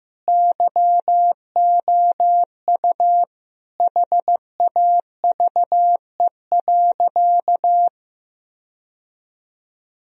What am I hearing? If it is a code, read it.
Morse code, heard as a keyed tone, about 16 wpm, 700 Hz. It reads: YOU HAVE.
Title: Sentences from Top 1000 Words - Review Part 3 - 15wpm